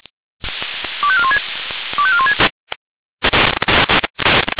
Five-tone selective call (ZVEI), heard as quick runs of five tones, tones of about 70 ms. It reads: 25416 25416
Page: /496/SDR/ZVEI/